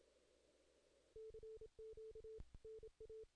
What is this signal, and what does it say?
Morse code, heard as a keyed tone, about 26 wpm, 440 Hz. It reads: CQ NA
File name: 2016 JAN CW NAQP / 2016 Jan naqp k5ot - 0-9